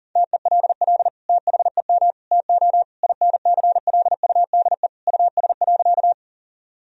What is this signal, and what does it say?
Morse code, heard as a keyed tone, about 40 wpm, 700 Hz. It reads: TELL THEM TO INCLUDE US.
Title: Sentences from Top 1000 Words - Review Part 3 - 40wpm